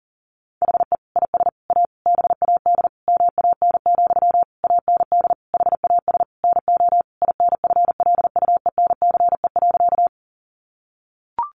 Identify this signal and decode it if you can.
Morse code, heard as a keyed tone, about 40 wpm, 700 Hz. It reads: HE IS A BAD MAN, AND HAS NO INFLUENCE.
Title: Sentences from Top 1100 Words 40wpm